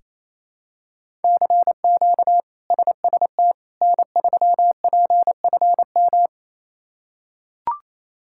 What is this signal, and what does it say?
Morse code, heard as a keyed tone, about 28 wpm, 700 Hz. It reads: CQ SST N3PFM
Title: K1USN SST QSO Practice - 28wpm